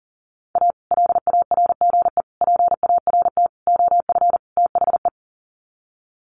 Morse code, heard as a keyed tone, about 40 wpm, 700 Hz. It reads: A LARGE PART OF THE
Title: Sets of 5 Words Encore2 40wpm